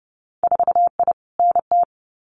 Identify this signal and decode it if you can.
Morse code, heard as a keyed tone, about 30 wpm, 700 Hz. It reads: 4I NT
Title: Winter Field Day Exchanges Encore 30wpm